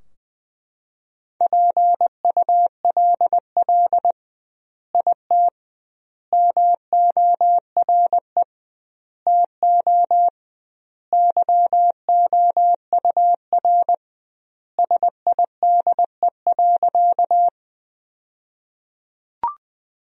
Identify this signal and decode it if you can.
Morse code, heard as a keyed tone, about 20 wpm, 700 Hz. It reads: PULL IT MORE TO YOUR SIDE.
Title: 2x Word-Spacing - Sentences from Top 400 Words - 20wpm